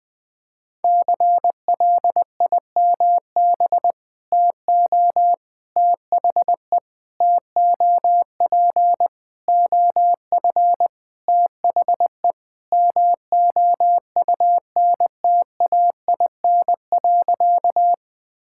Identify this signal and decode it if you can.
Morse code, heard as a keyed tone, about 20 wpm, 700 Hz. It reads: CLIMB TO THE TOP OF THE MOUNTAIN.
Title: Sentences with 5-Word Phrases - 20wpm